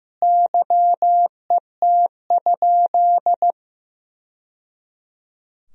Morse code, heard as a keyed tone, about 16 wpm, 700 Hz. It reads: YET?
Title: Sentences from Top 1000 Words - Review Part 4 - 15wpm